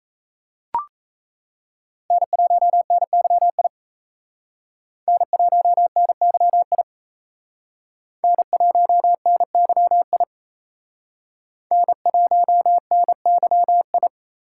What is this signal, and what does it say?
Morse code, heard as a keyed tone, about 34 wpm, 700 Hz. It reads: N1NYI N1NYI N1NYI N1NYI
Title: Speed Racing - USA 1x3 Call Signs 28wpm